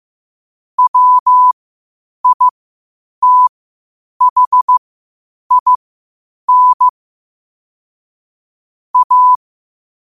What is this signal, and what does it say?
Morse code, heard as a keyed tone, about 15 wpm, spaced as for 8 wpm, 1000 Hz. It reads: WITHIN A
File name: qotdfiles / 8wpm